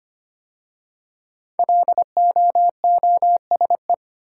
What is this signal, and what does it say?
Morse code, heard as a keyed tone, about 25 wpm, 700 Hz. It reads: LOOSE
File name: 5 Letter Words - First 3 Letters Given - 25wpm